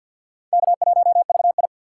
Morse code, heard as a keyed tone, about 52 wpm, 700 Hz. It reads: K1VI
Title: USA Call Signs - 4 Characters - 3x Character Spacing - 50wpm